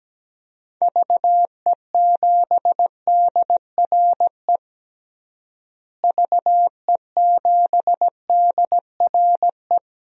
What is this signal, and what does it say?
Morse code, heard as a keyed tone, about 17 wpm, 700 Hz. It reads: VE7DRE VE7DRE
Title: DXCC-100 Call Signs - 6 Characters - Repeated 2x - 17wpm